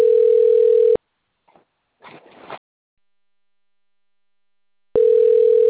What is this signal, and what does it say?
Ringback tone (North America), 440 + 480 Hz.